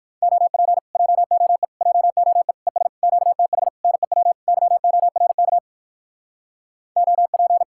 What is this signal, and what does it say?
Morse code, heard as a keyed tone, about 64 wpm, 700 Hz. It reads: OP JOE JOE ES QTH NEW YORK OP